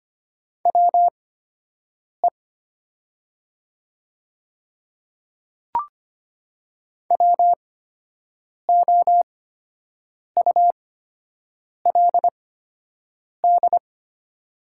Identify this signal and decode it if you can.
Morse code, heard as a keyed tone, about 25 wpm, spaced as for 7 wpm, 700 Hz. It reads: WE WOULD
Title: 8x Character Spacing - Top 100 Words - 25wpm